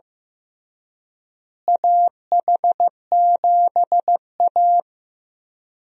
Morse code, heard as a keyed tone, about 15 wpm, 700 Hz. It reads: AH7A
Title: USA Call Signs - 4 Characters - Repeated 2x - 15wpm